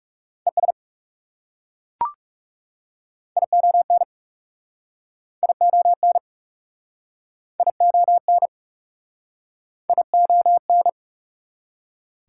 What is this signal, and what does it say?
Morse code, heard as a keyed tone, about 39 wpm, 700 Hz. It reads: ES ION ION ION ION